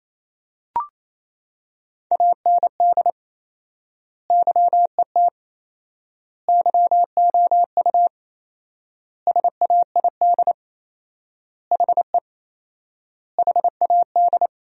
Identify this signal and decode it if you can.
Morse code, heard as a keyed tone, about 28 wpm, 700 Hz. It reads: AND YET YOU SAID HE HAD
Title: 4x Word-Spacing - Sentences from Top 500 Words - 28wpm